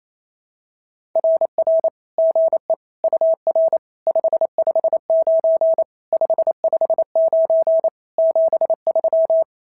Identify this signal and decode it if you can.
Morse code, heard as a keyed tone, about 28 wpm, 650 Hz. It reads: RR GE UR 559 559 73
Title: SOTA QSO Practice 28wpm